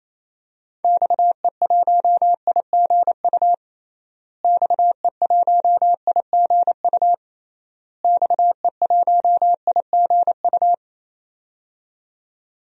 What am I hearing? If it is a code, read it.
Morse code, heard as a keyed tone, about 28 wpm, 700 Hz. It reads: XE1IGU XE1IGU XE1IGU
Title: DXCC-100 Call Signs - 6 Characters - Repeated 3x - 28wpm